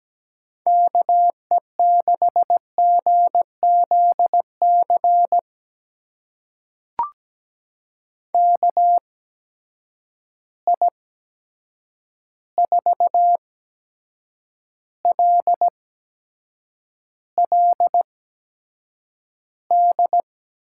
Morse code, heard as a keyed tone, about 17 wpm, spaced as for 7 wpm, 700 Hz. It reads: KE6GZC KI4LLD
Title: USA Call Signs - 6 Characters - 8x Character Spacing - 17wpm